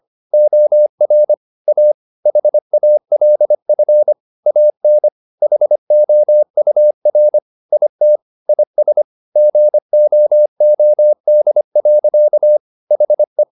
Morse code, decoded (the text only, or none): OR A HALF AN HOUR IT IS GOOD. HE